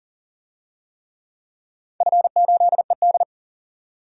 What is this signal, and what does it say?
Morse code, heard as a keyed tone, about 40 wpm, 700 Hz. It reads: F8ED
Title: DXCC-100 Call Signs - 4 Characters - 40wpm